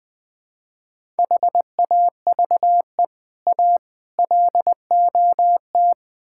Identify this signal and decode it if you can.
Morse code, heard as a keyed tone, about 20 wpm, 700 Hz. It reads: HAVE A LOT
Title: Build A Phrase - 4-Word Phrases - 20wpm